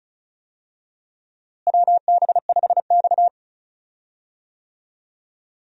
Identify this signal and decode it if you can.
Morse code, heard as a keyed tone, about 36 wpm, 700 Hz. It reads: WB5X
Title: USA Call Signs - 4 Characters - Repeated 3x - 35wpm